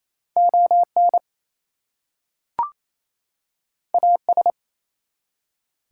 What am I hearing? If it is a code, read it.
Morse code, heard as a keyed tone, about 27 wpm, 700 Hz. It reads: ON AS
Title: Two Letter Words - 28wpm